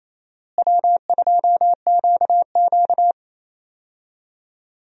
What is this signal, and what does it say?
Morse code, heard as a keyed tone, about 28 wpm, 700 Hz. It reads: W2QQ
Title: USA Call Signs - 4 Characters - Repeated 3x - 28wpm